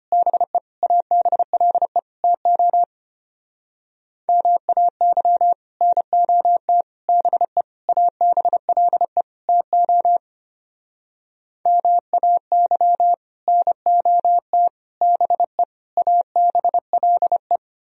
Morse code, heard as a keyed tone, about 33 wpm, 700 Hz. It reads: BE ABLE TO MAY NOT BE ABLE TO MAY NOT BE ABLE